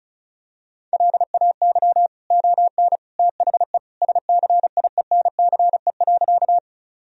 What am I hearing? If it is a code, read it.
Morse code, heard as a keyed tone, about 36 wpm, 700 Hz. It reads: LAY ON THE SCIENCE.
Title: Sentences from Top 1000 Words - Review Part 2 - 35wpm